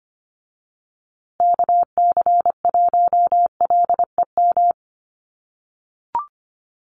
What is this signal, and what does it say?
Morse code, heard as a keyed tone, about 25 wpm, 700 Hz. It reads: KC1LEM
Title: USA Call Signs Repeated 3x 25wpm